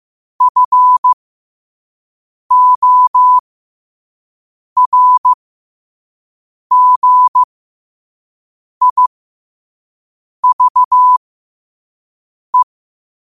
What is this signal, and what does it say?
Morse code, heard as a keyed tone, about 15 wpm, spaced as for 5 wpm, 1000 Hz. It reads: FORGIVE
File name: 5wpm